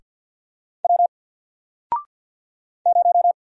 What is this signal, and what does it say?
Morse code, heard as a keyed tone, about 48 wpm, 700 Hz. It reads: W 0
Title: Single Character (Letter:Number:Punctuation:Prosign) Alt 50wpm